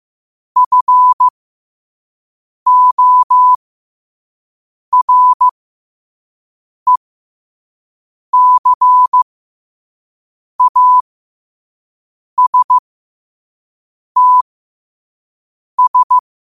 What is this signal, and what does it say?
Morse code, heard as a keyed tone, about 15 wpm, spaced as for 5 wpm, 1000 Hz. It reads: FORECASTS
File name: qotdfiles / 5wpm